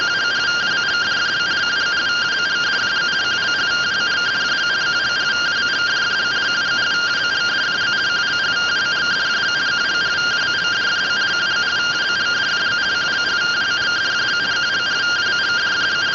Two-tone teleprinter signal, about 170 Hz shift.